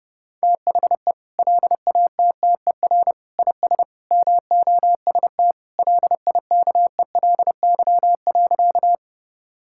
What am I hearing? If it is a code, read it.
Morse code, heard as a keyed tone, about 32 wpm, 700 Hz. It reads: THE LATTER IS MOST LIKELY.
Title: Sentences from Top 1600 Words 30wpm